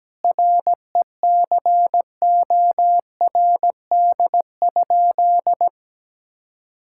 Morse code, heard as a keyed tone, about 17 wpm, 700 Hz. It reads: RECORD?
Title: Sentences from Top 1000 Words - Review Part 6 - 17wpm